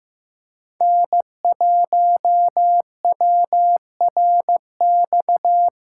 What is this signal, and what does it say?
Morse code, heard as a keyed tone, about 15 wpm, 700 Hz. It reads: N1WRX
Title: USA Callsign DE Callsign - 5 Characters - 15wpm